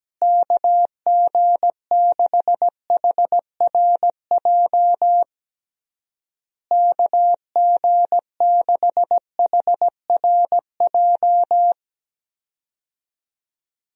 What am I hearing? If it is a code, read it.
Morse code, heard as a keyed tone, about 17 wpm, 700 Hz. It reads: KG6HRJ KG6HRJ
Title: USA Call Signs - 6 Characters - Repeated 3x - 17wpm